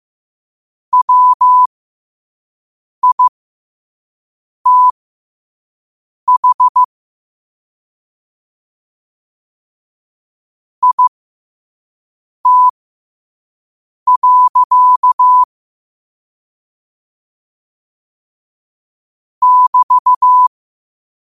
Morse code, heard as a keyed tone, about 15 wpm, spaced as for 5 wpm, 1000 Hz. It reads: WITH IT. <BT>
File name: qotdfiles / 5wpm